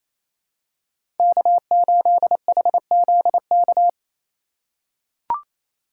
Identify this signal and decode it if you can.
Morse code, heard as a keyed tone, about 28 wpm, 700 Hz. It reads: K8HZK
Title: USA Call Signs - 5 Characters - 8x Character Spacing - 28wpm